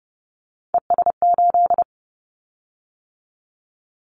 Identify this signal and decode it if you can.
Morse code, heard as a keyed tone, about 30 wpm, 700 Hz. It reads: ES8
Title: Callsign Prefixes for Top 50 DXCC - 30wpm